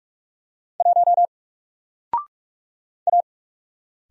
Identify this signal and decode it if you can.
Morse code, heard as a keyed tone, about 44 wpm, 700 Hz. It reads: 1 A
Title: Single Letter-Number 45wpm